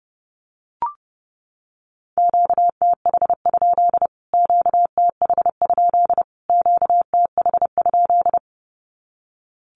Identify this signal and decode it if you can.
Morse code, heard as a keyed tone, about 30 wpm, 700 Hz. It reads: QTH? QTH? QTH?
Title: Q-Codes Repeated 3x 30wpm